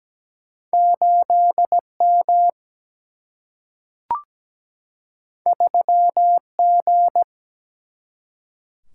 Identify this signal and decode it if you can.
Morse code, heard as a keyed tone, about 17 wpm, 700 Hz. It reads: 8M 3G
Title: Single-Letter Single-Number 17wpm